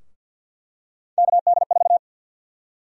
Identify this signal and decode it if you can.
Morse code, heard as a keyed tone, about 51 wpm, 700 Hz. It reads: KD4
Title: USA Call Sign Prefixes - Repeated 2x - 50wpm